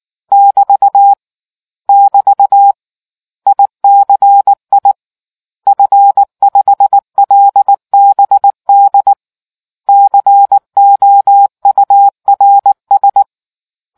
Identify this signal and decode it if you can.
Morse code, heard as a keyed tone, about 19 wpm, 800 Hz. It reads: <BT> <BT> ICI F5LBD COURS